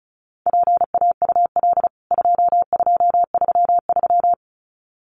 Morse code, heard as a keyed tone, about 34 wpm, 700 Hz. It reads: PAUL 2233
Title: CWT Exchange 35wpm